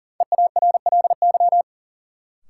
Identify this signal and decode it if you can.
Morse code, heard as a keyed tone, about 40 wpm, 700 Hz. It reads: EARLY
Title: Sets of 3 Words - New Word Drill - 40wpm